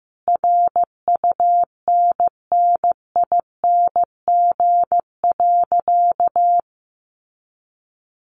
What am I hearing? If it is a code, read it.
Morse code, heard as a keyed tone, about 15 wpm, 700 Hz. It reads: RUNNING.
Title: Additional Easy Phrases 15wpm